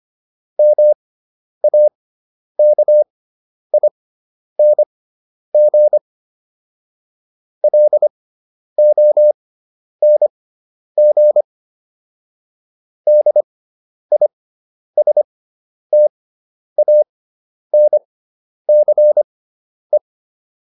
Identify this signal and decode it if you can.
Morse code, heard as a keyed tone, about 25 wpm, spaced as for 10 wpm, 600 Hz. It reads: MAKING LONG DISTANCE